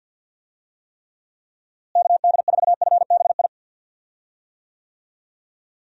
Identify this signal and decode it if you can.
Morse code, heard as a keyed tone, about 52 wpm, 700 Hz. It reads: KD4FBI
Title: USA Call Signs - 6 Characters - Repeated 3x - 50wpm